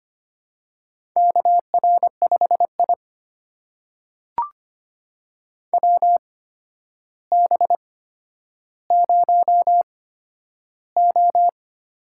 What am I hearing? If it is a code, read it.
Morse code, heard as a keyed tone, about 25 wpm, spaced as for 11 wpm, 700 Hz. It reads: KR5I WB0O
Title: USA Call Signs - 4 Characters - 8x Character Spacing - 25wpm